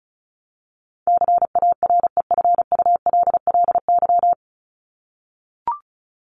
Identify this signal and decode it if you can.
Morse code, heard as a keyed tone, about 34 wpm, 700 Hz. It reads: CAREFULLY
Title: Top 1001-1250 Words 35wpm